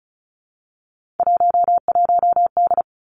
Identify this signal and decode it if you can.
Morse code, heard as a keyed tone, about 36 wpm, 700 Hz. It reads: 11D
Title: Three Challenging Letter-Number Combinations 35wpm